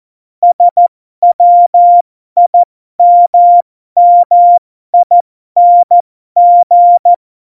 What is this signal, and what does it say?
Morse code, heard as a keyed tone, about 14 wpm, spaced as for 12 wpm, 700 Hz. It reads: SWIMMING